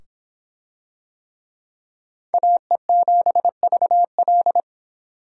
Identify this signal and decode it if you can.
Morse code, heard as a keyed tone, about 22 wpm, 700 Hz. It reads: AE7VL